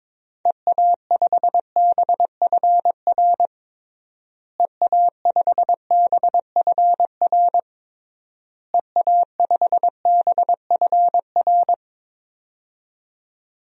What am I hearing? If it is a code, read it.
Morse code, heard as a keyed tone, about 21 wpm, 700 Hz. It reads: EA5BFR EA5BFR EA5BFR
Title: DXCC-100 Call Signs - 6 Characters - Repeated 3x - 22wpm